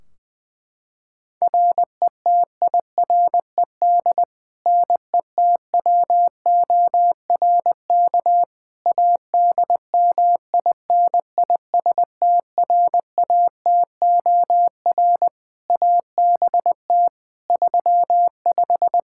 Morse code, heard as a keyed tone, about 20 wpm, 700 Hz. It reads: RETIRED NETWORK ADMINISTRATOR ABT 35
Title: Practice QSO - Exchange 3 - More Info - 20wpm